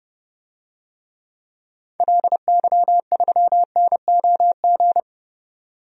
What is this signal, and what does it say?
Morse code, heard as a keyed tone, about 30 wpm, 700 Hz. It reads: LY3NOG
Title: DXCC-100 Call Signs - 6 Characters - 30wpm